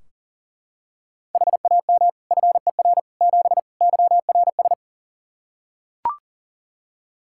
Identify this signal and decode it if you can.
Morse code, heard as a keyed tone, about 40 wpm, 700 Hz. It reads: HAM FER 7 YRS
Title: Practice QSO - Exchange 3 - More Info - 40wpm